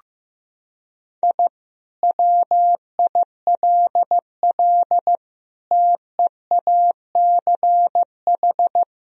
Morse code, heard as a keyed tone, about 15 wpm, 700 Hz. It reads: I WILL TEACH
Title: Sentences from Top 1000 Words - Review Part 3 - 15wpm